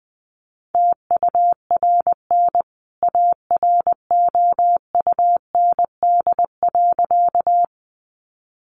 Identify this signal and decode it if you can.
Morse code, heard as a keyed tone, about 20 wpm, 700 Hz. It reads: TURN AROUND.
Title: A Sound Of Thunder 20wpm